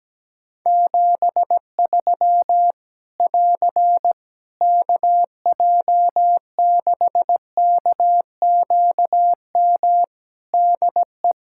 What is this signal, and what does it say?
Morse code, heard as a keyed tone, about 17 wpm, 700 Hz. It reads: 73 <AR> KJ6KQM DE
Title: Practice QSO - Exchange 4 - Ending - 17wpm